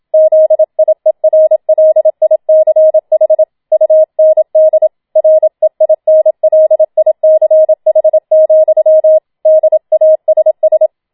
Morse code, heard as a keyed tone, about 27 wpm, 630 Hz. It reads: ZIERLICH UND REINLICH, DASS
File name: Schneewittchen Teil2 27wpm